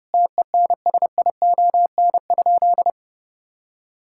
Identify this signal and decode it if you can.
Morse code, heard as a keyed tone, about 30 wpm, 700 Hz. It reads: TENSION?
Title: Sentences from Top 1600 Words 30wpm